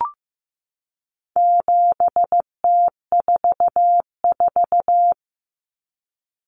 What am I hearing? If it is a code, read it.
Morse code, heard as a keyed tone, about 15 wpm, 700 Hz. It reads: 7T44
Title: Common Cut Numbers - 4-Digit 15wpm